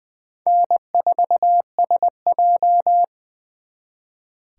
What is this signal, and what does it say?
Morse code, heard as a keyed tone, about 24 wpm, 700 Hz. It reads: N4SJ
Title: Speed Racing - USA 1x2 Call Signs 20wpm